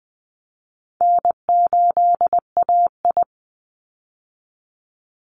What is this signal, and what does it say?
Morse code, heard as a keyed tone, about 20 wpm, 700 Hz. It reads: N8AI
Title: Easy Call Signs - 20wpm